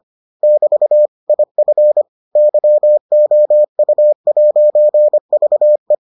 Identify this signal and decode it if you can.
Morse code, heard as a keyed tone, about 25 wpm, 600 Hz. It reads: <BT> IF YOU'VE